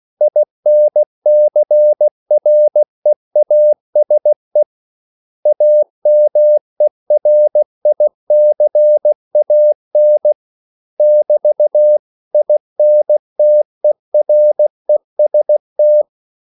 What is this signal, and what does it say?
Morse code, heard as a keyed tone, about 16 wpm, 600 Hz. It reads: INCREASE AMERICAN <BT>INTEREST
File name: CW / Practice / 500MostCommonEnglishWords16